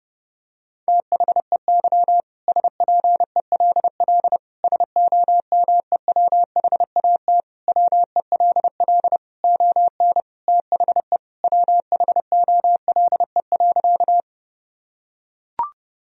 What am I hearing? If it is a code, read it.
Morse code, heard as a keyed tone, about 30 wpm, 700 Hz. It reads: THEY SPELL SOMEWHAT WELL ON THE WHOLE.
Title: Sentences from Top 1000 Words - Review Part 1 - 30wpm